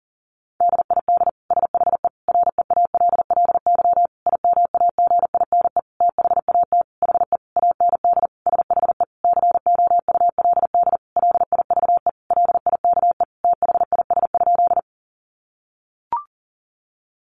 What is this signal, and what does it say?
Morse code, heard as a keyed tone, about 40 wpm, 700 Hz. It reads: DID SHE REALLY IMAGINE THAT HE AND SHE COULD LIVE LIKE THIS?
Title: Sentences from Top 1400 Words 40wpm